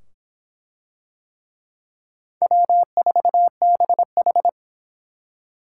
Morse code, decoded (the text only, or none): W4BH